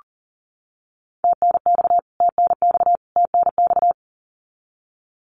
Morse code, heard as a keyed tone, about 38 wpm, 700 Hz. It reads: TNX TNX TNX
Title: Abbreviations Repeated 3x - 40wpm